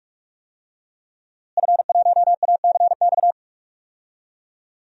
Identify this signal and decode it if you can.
Morse code, heard as a keyed tone, about 42 wpm, 700 Hz. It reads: F1ACX
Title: DXCC-100 Call Signs - 5 Characters - Repeated 3x - 45wpm